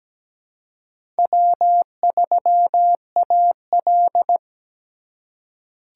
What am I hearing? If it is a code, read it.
Morse code, heard as a keyed tone, about 17 wpm, 700 Hz. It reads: W3AL
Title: USA Call Signs - 4 Characters - 2x Character Spacing - 17wpm